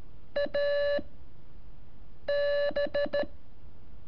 Morse code, heard as a keyed tone, about 13 wpm, 580 Hz. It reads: A B
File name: curso / ejercicio 17